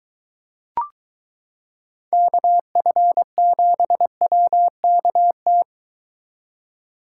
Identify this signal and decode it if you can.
Morse code, heard as a keyed tone, about 22 wpm, 700 Hz. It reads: KF7WKT